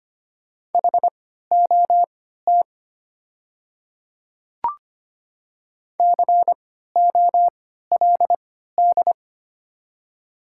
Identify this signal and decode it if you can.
Morse code, heard as a keyed tone, about 25 wpm, spaced as for 14 wpm, 700 Hz. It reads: HOT COLD
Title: 3x Character Spacing - Top 500 Words - 25wpm